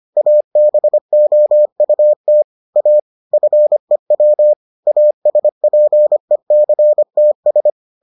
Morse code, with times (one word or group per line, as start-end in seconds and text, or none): 0.2-2.4 ABOUT
2.8-3.0 A
3.3-4.5 FEW
4.9-7.7 ASPECTS